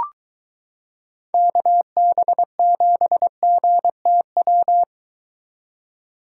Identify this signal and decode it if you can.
Morse code, heard as a keyed tone, about 22 wpm, 700 Hz. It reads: KB7GTW